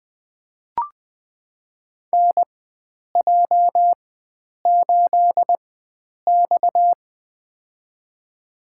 Morse code, heard as a keyed tone, about 20 wpm, spaced as for 9 wpm, 700 Hz. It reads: NJ8X
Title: USA Call Signs - 4 Characters - 4x Character Spacing - 20wpm